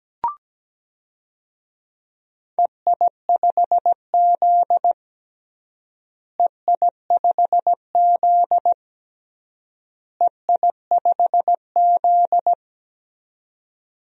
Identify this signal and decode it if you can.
Morse code, heard as a keyed tone, about 17 wpm, 700 Hz. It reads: EI5Z EI5Z EI5Z
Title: DXCC-100 Call Signs - 4 Characters - Repeated 3x - 17wpm